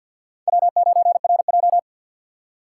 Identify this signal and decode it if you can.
Morse code, heard as a keyed tone, about 49 wpm, 700 Hz. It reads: W9RJ
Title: USA Call Signs - 4 Characters - 3x Character Spacing - 50wpm